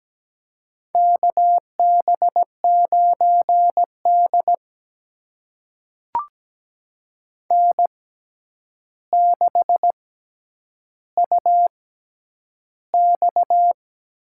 Morse code, heard as a keyed tone, about 17 wpm, spaced as for 9 wpm, 700 Hz. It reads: KB9D N6UX
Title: USA Call Signs - 4 Characters - 6x Character Spacing - 17wpm